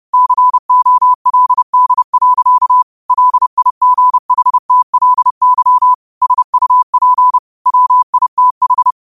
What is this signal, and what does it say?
Morse code, heard as a keyed tone, about 30 wpm, 1000 Hz. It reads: COLD. LIGHTLY SUP WITH
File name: qotdfiles / 30wpm